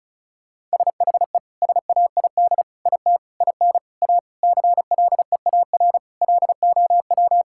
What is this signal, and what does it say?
Morse code, heard as a keyed tone, about 36 wpm, 700 Hz. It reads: SHE SAID IT IN A CLEAR LOW
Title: Sentences from Top 1600 Words 35wpm